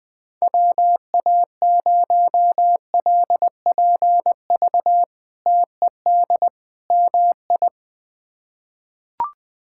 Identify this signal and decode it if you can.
Morse code, heard as a keyed tone, about 20 wpm, 700 Hz. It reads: WA0LPV TED MI
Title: K1USN SST QSO Practice - 20wpm